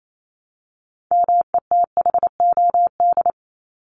Morse code, heard as a keyed tone, about 28 wpm, 700 Hz. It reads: METHOD
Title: Top 501-750 Words 28wpm